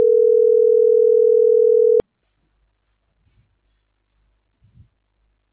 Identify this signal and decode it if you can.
Ringback tone (North America), 440 + 480 Hz.